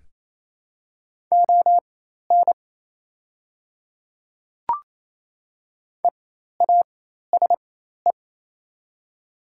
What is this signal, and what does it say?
Morse code, heard as a keyed tone, about 28 wpm, spaced as for 13 wpm, 700 Hz. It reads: ON EASE
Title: 4x Character Spacing - Top 300 Words - 28wpm